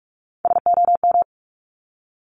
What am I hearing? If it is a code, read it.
Morse code, heard as a keyed tone, about 46 wpm, 700 Hz. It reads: SOM
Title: Top Three Letter Combinations in Words 45wpm